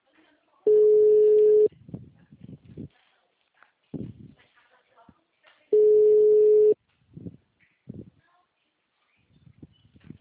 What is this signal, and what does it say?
Ringback tone (Europe, 425 Hz), with speech.
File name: out- -2022- - - .8524